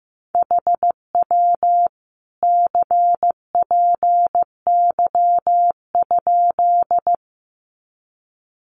Morse code, heard as a keyed tone, about 15 wpm, 700 Hz. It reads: HW CPY?
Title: Common QSO Phrases 15wpm